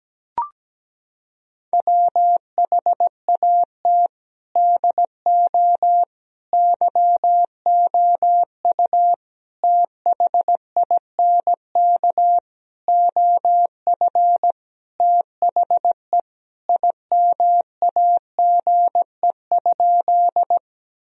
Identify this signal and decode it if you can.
Morse code, heard as a keyed tone, about 17 wpm, 700 Hz. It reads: WHAT DO YOU THINK OF THE IMAGE?
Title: Sentences from Top 1000 Words - Review Part 4 - 17wpm